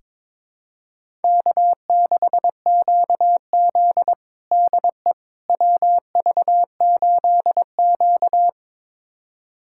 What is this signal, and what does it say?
Morse code, heard as a keyed tone, about 23 wpm, 700 Hz. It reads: K6QZ DE WV8Q
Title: USA Callsign DE Callsign - 4 Characters - 22wpm